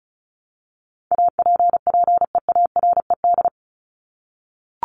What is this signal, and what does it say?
Morse code, heard as a keyed tone, about 34 wpm, 700 Hz. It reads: APPEARED
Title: ED Suffix - Top 100 Words with Base Words 35wpm